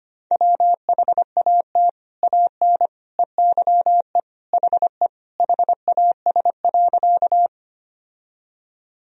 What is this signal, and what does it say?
Morse code, heard as a keyed tone, about 25 wpm, 700 Hz. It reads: WHAT AN EYE HE HAS.
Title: Sentences from Top 1000 Words - Review Part 1 - 25wpm